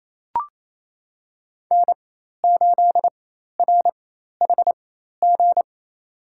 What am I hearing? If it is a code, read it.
Morse code, heard as a keyed tone, about 28 wpm, spaced as for 13 wpm, 700 Hz. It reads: N8RHG
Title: USA Call Signs - 5 Characters - 4x Character Spacing - 28wpm